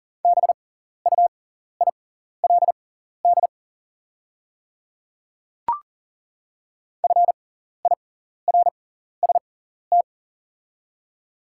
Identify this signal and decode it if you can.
Morse code, heard as a keyed tone, about 40 wpm, spaced as for 14 wpm, 700 Hz. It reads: BUILD FIRST
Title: 6x Character Spacing - Top 200 Words - 40wpm